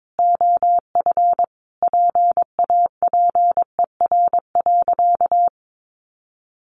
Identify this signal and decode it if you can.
Morse code, heard as a keyed tone, about 22 wpm, 700 Hz. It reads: OF PAPER.
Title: Sentences from Top 600 Words 22wpm